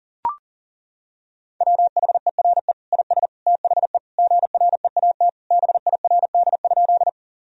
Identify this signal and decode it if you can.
Morse code, heard as a keyed tone, about 40 wpm, 700 Hz. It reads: WHERE IS THE GREAT BIRD?
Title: Sentences from Top 1000 Words - Review Part 2 - 40wpm